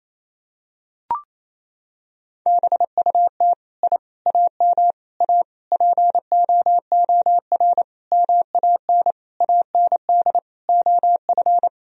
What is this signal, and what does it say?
Morse code, heard as a keyed tone, about 28 wpm, 700 Hz. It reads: BUT I AM A POOR MAN AND OF
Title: Sentences from Top 1000 Words - Review Part 3 - 28wpm